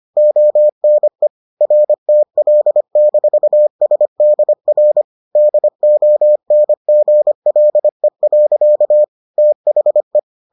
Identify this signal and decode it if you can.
Morse code, heard as a keyed tone, about 25 wpm, 600 Hz. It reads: ONE RTL-SDR DONGLE. THE